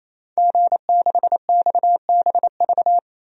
Morse code, heard as a keyed tone, about 28 wpm, 700 Hz. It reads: G6XBV
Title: DXCC-100 Call Signs - 5 Characters - 28wpm